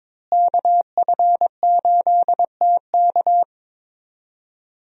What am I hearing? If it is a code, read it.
Morse code, heard as a keyed tone, about 22 wpm, 700 Hz. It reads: KF8TK
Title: USA Call Signs - 5 Characters - Repeated 3x - 22wpm